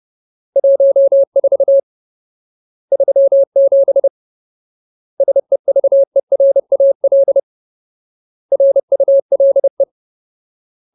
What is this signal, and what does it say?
Morse code, heard as a keyed tone, about 30 wpm, 550 Hz. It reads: 14 37 SEVERAL RULE
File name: Ultimate Word List / 30WPM Ultimate